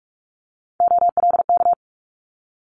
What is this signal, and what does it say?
Morse code, heard as a keyed tone, about 46 wpm, 700 Hz. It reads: OLK
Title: Multi-letter Phonemes Quick Fire 45wpm